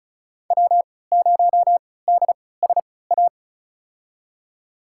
Morse code, heard as a keyed tone, about 36 wpm, spaced as for 20 wpm, 700 Hz. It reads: W0DSA
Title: USA Call Signs - 3x Character Spacing - 35wpm